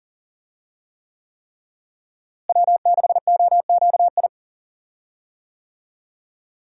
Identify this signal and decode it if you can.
Morse code, heard as a keyed tone, about 40 wpm, 700 Hz. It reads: W6OQI
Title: USA Call Signs - 5 Characters - Repeated 3x - 40wpm